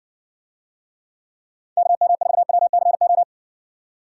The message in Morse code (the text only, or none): KM4WCO